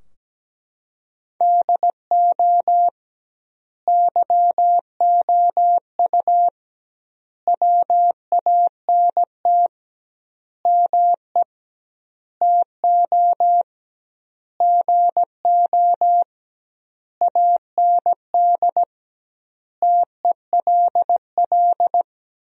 Morse code, heard as a keyed tone, about 17 wpm, 700 Hz. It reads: DO YOU WANT ME TO GO AND TELL
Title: 2x Word-Spacing - Sentences from Top 200 Words - 17wpm